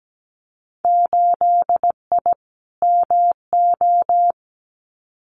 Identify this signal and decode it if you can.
Morse code, heard as a keyed tone, about 17 wpm, 700 Hz. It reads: 8I MO
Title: Winter Field Day Exchanges Encore 17wpm